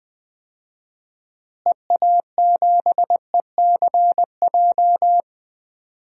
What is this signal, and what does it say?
Morse code, heard as a keyed tone, about 20 wpm, 700 Hz. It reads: EA7ECJ
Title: DXCC-100 Call Signs - 6 Characters - Repeated 3x - 20wpm